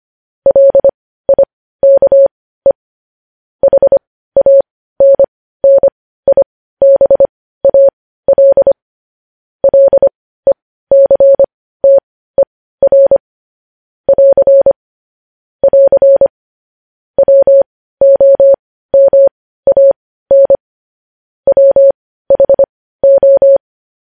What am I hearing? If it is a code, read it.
Morse code, heard as a keyed tone, about 25 wpm, spaced as for 15 wpm, 550 Hz. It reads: LIKE HANNIBAL LECTER <AR> <AR> WOMAN WHO
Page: news headlines in morse code at 15 wpm